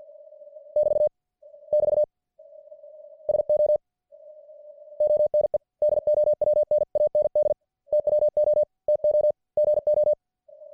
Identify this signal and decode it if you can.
Morse code, heard as a keyed tone, about 50 wpm, 600 Hz. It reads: <BT> <BT> SO ONE DOWN AND TWO TO GO